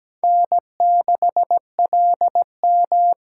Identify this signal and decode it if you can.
Morse code, heard as a keyed tone, about 20 wpm, 700 Hz. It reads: N6LM
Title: Speed Racing - USA 1x2 Call Signs 17wpm